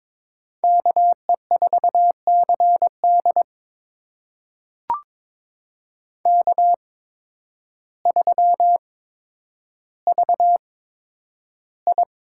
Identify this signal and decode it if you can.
Morse code, heard as a keyed tone, about 21 wpm, spaced as for 10 wpm, 700 Hz. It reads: KE4CD K3VI
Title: USA Call Signs - 8x Character Spacing - 22wpm